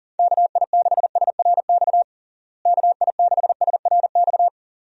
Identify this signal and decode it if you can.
Morse code, heard as a keyed tone, about 40 wpm, 700 Hz. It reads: KI6SRX KI6SRX
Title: USA Call Signs - 6 Characters - Repeated 2x - 40wpm